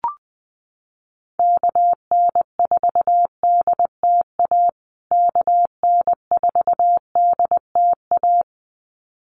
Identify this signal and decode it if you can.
Morse code, heard as a keyed tone, about 20 wpm, 700 Hz. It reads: KN4DTA KN4DTA
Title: USA Call Signs Repeated 2x 20wpm